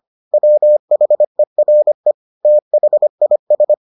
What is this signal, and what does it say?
Morse code, heard as a keyed tone, about 25 wpm, 600 Hz. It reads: WHERE THIS